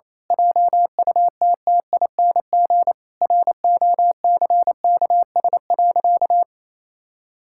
Morse code, heard as a keyed tone, about 28 wpm, 700 Hz. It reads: JUTTING ROCKS.
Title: Pangram Challenge v2 - 28wpm